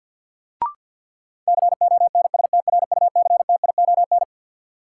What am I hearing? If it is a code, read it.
Morse code, heard as a keyed tone, about 51 wpm, 700 Hz. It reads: CONSTRUCTION
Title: Sentences from Top 1600 Words - New Word Drill - 50wpm